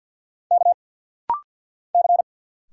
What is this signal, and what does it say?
Morse code, heard as a keyed tone, about 50 wpm, 700 Hz. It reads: K C
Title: Single Letters - Rapid-Fire 50wpm